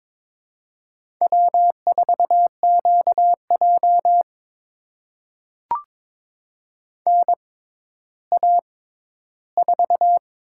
Morse code, heard as a keyed tone, about 22 wpm, spaced as for 12 wpm, 700 Hz. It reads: W4QJ NA4
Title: USA Call Signs - 4 Characters - 6x Character Spacing - 22wpm